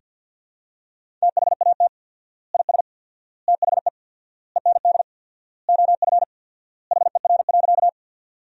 Morse code, heard as a keyed tone, about 50 wpm, 700 Hz. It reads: THAT IS THE END OF HER.
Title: 4x Word-Spacing - Sentences from Top 200 Words - 50wpm